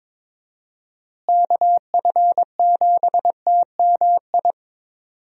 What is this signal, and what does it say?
Morse code, heard as a keyed tone, about 22 wpm, 700 Hz. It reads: KF7TMI
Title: USA Call Signs - 6 Characters - Repeated 3x - 22wpm